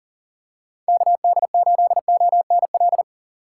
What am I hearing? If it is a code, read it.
Morse code, heard as a keyed tone, about 38 wpm, 700 Hz. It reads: KD8ONL
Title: USA Call Signs - 6 Characters - 8x Character Spacing - 40wpm